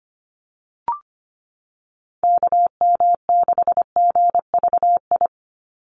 Morse code, heard as a keyed tone, about 25 wpm, 700 Hz. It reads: KM6GVI
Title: USA Call Signs 25wpm